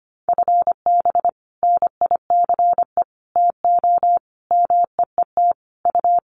Morse code, heard as a keyed tone, about 25 wpm, 700 Hz. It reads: FB NICE TO MEET U
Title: Common QSO Phrases 25wpm